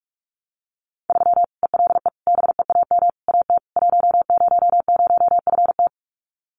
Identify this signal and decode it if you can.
Morse code, heard as a keyed tone, about 44 wpm, 700 Hz. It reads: 3 ELE BEAM AT 100FT
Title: Common QSO Phrases 45wpm